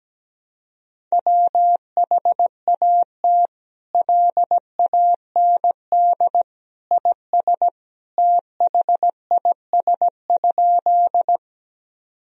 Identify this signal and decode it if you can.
Morse code, heard as a keyed tone, about 17 wpm, 700 Hz. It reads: WHAT LAND IS THIS?
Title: Sentences from Top 1000 Words - Review Part 1 - 17wpm